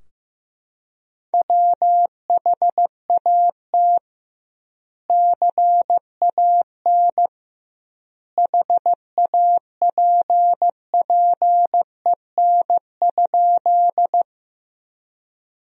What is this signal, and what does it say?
Morse code, heard as a keyed tone, about 15 wpm, 700 Hz. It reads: WHAT CAN HAPPEN?
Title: 2x Word-Spacing - Sentences from Top 400 Words - 15wpm